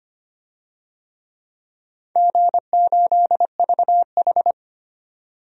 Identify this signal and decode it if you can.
Morse code, heard as a keyed tone, about 25 wpm, 700 Hz. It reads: G8VH
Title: DXCC-100 Call Signs - 4 Characters - 25wpm